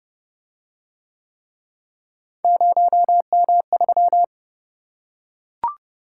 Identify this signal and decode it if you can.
Morse code, heard as a keyed tone, about 29 wpm, 700 Hz. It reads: 0M3